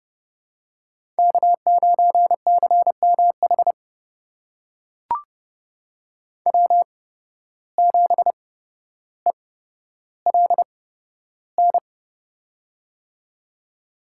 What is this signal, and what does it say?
Morse code, heard as a keyed tone, about 29 wpm, spaced as for 13 wpm, 700 Hz. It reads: K9CMH W7ELN
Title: USA Call Signs - 5 Characters - 8x Character Spacing - 30wpm